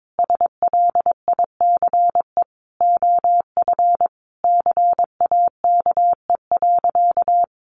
Morse code, heard as a keyed tone, about 22 wpm, 700 Hz. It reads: SLICE OF CAKE.
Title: Additional Easy Phrases 22wpm